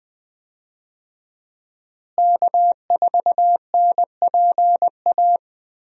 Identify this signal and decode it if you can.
Morse code, heard as a keyed tone, about 20 wpm, 700 Hz. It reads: K4NPA
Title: USA Call Signs - 5 Characters - Repeated 3x - 20wpm